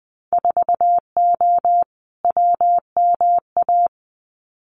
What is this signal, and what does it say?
Morse code, heard as a keyed tone, about 20 wpm, 700 Hz. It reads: 4O WMA
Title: Winter Field Day Exchanges Encore 20wpm